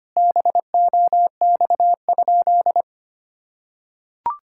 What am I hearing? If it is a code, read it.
Morse code, heard as a keyed tone, about 25 wpm, 700 Hz. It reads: BOX?
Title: Sentences from Top 1000 Words - Review Part 5 - 25wpm